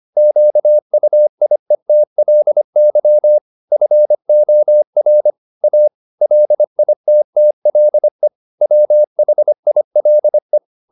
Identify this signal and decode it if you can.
Morse code, heard as a keyed tone, about 25 wpm, 600 Hz. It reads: QUIETLY FOR A LITTLE WHILE